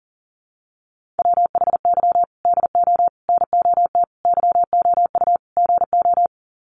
Morse code, heard as a keyed tone, about 40 wpm, 700 Hz. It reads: WHY DO NOT YOU GO
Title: Sets of 5 Words Encore 40wpm